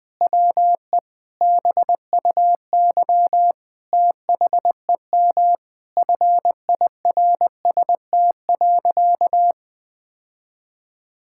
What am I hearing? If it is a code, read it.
Morse code, heard as a keyed tone, about 21 wpm, 700 Hz. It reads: WE BUY THEM FIRST.
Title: Sentences from Top 1000 Words - Review Part 6 - 20wpm